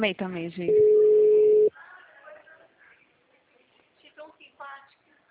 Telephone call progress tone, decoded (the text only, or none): ringback tone